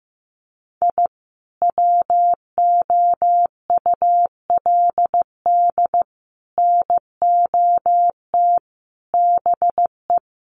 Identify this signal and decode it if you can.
Morse code, heard as a keyed tone, about 15 wpm, 700 Hz. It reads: I WOULD NOT BE
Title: Sets of 5 Words Encore2 15wpm